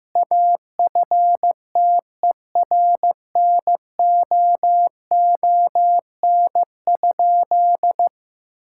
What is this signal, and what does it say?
Morse code, heard as a keyed tone, about 15 wpm, 700 Hz. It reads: AFTERNOON?
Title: Sentences from Top 1600 Words 15wpm